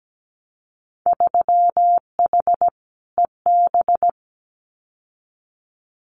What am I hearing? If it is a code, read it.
Morse code, heard as a keyed tone, about 17 wpm, 700 Hz. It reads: 3H EB
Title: Winter Field Day Exchanges Encore 17wpm